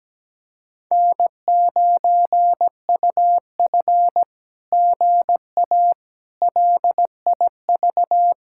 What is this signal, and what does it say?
Morse code, heard as a keyed tone, about 17 wpm, 700 Hz. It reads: N9UF GA LIV
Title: SOTA QSO Practice 17wpm